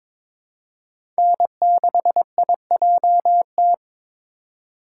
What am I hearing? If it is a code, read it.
Morse code, heard as a keyed tone, about 21 wpm, 700 Hz. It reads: N6IJT
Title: USA Call Signs - 5 Characters - 3x Character Spacing - 22wpm